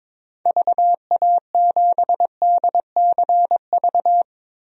Morse code, heard as a keyed tone, about 22 wpm, 700 Hz. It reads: VA7DCV
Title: DXCC-100 Call Signs - 6 Characters - 22wpm